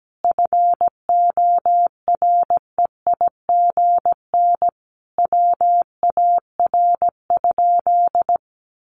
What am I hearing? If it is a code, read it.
Morse code, heard as a keyed tone, about 17 wpm, 700 Hz. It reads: FOREIGN WAR?
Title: Sentences from Top 1400 Words 17wpm